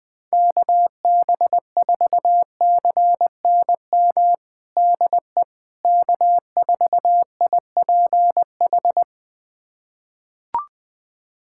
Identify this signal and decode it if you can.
Morse code, heard as a keyed tone, about 20 wpm, 700 Hz. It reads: KB4CNM DE K4IPH
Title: USA Callsign DE Callsign 20wpm